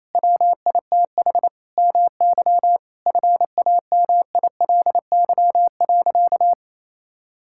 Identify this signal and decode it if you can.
Morse code, heard as a keyed tone, about 28 wpm, 700 Hz. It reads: WITH MY FAMILY.
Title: Sentences from Top 1600 Words 28wpm